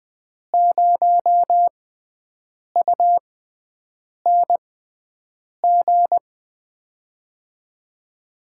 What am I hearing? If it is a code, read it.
Morse code, heard as a keyed tone, about 21 wpm, spaced as for 7 wpm, 700 Hz. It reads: 0UNG
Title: USA Call Signs - 6 Characters - 6x Character Spacing - 20wpm